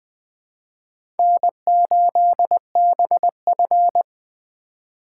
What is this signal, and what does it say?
Morse code, heard as a keyed tone, about 20 wpm, 700 Hz. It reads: N8BF
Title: USA Call Signs - 4 Characters - 4x Character Spacing - 20wpm